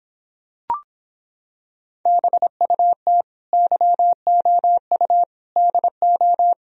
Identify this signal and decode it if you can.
Morse code, heard as a keyed tone, about 26 wpm, 700 Hz. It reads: BUT YOU DO